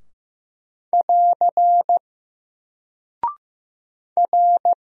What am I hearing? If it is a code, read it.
Morse code, heard as a keyed tone, about 15 wpm, 700 Hz. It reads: <AR> R
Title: Single Character (Letter:Number:Punctuation:Prosign) Alt 15wpm